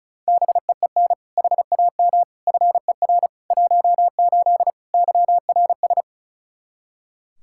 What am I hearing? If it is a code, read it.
Morse code, heard as a keyed tone, about 34 wpm, 700 Hz. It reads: BEEN HAM FER 18 YRS
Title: Practice QSO - Exchange 3 - More Info - 35wpm